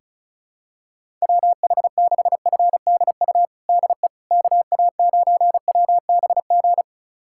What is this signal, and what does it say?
Morse code, heard as a keyed tone, about 36 wpm, 700 Hz. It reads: WH6FDU DE KA9WBG
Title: USA Callsign DE Callsign - 6 Characters - 35wpm